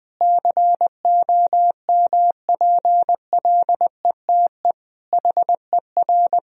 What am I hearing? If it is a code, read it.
Morse code, heard as a keyed tone, about 20 wpm, 700 Hz. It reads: COMPLETE HER
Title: Sentences from Top 1000 Words - Review Part 2 - 20wpm